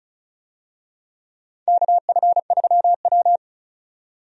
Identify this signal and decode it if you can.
Morse code, heard as a keyed tone, about 36 wpm, 700 Hz. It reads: KF3W
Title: USA Call Signs - 4 Characters - 35wpm